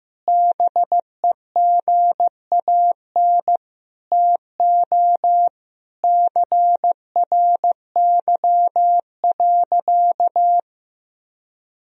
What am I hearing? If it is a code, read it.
Morse code, heard as a keyed tone, about 16 wpm, 700 Hz. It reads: BEGAN TO CRY.
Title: Sentences from Top 1000 Words - Review Part 2 - 15wpm